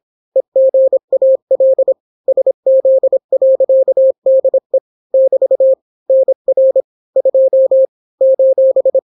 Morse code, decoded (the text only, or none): EGAL SZ.DE <BT> NR 2 :